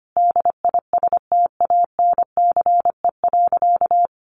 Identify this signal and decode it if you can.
Morse code, heard as a keyed tone, about 25 wpm, 700 Hz. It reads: DISTANCE.
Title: Sentences from Top 1200 Words 25wpm